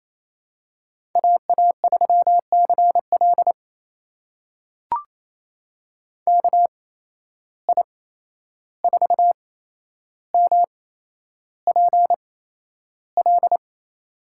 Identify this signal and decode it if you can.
Morse code, heard as a keyed tone, about 28 wpm, spaced as for 11 wpm, 700 Hz. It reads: AA3CL KI4MPL
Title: USA Call Signs - 8x Character Spacing - 28wpm